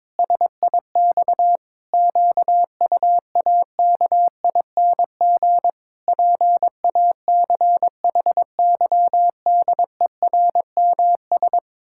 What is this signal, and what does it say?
Morse code, heard as a keyed tone, about 22 wpm, 700 Hz. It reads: SIX QUAKING PACHYDERMS
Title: Pangram Challenge v2 - 22wpm